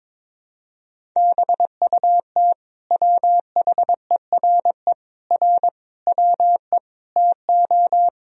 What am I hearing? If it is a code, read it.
Morse code, heard as a keyed tone, about 22 wpm, 700 Hz. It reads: BUT WHERE R WE TO
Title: Sentences from Top 200 Words and Abbreviations 22wpm